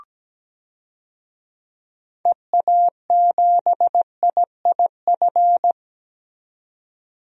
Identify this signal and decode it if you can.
Morse code, heard as a keyed tone, about 17 wpm, 700 Hz. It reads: EA7IIF
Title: DXCC-100 Call Signs - 6 Characters - 17wpm